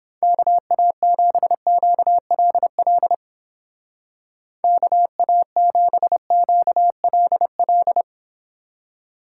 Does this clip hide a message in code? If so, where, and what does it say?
Yes: Morse code from 0.2 to 8.0 s, heard as a keyed tone, about 29 wpm, 700 Hz. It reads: KA7QLL KA7QLL